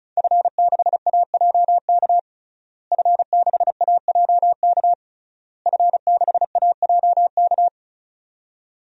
Morse code, heard as a keyed tone, about 36 wpm, 700 Hz. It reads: F6AJK F6AJK F6AJK
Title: DXCC-100 Call Signs - 5 Characters - Repeated 3x - 35wpm